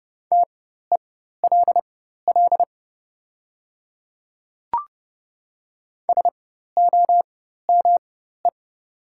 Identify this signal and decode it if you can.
Morse code, heard as a keyed tone, about 30 wpm, spaced as for 14 wpm, 700 Hz. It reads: TELL SOME
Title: 4x Character Spacing - Top 200 Words - 30wpm